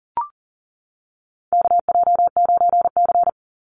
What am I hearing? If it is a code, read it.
Morse code, heard as a keyed tone, about 40 wpm, 700 Hz. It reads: KJ9C
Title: Call Signs - 40wpm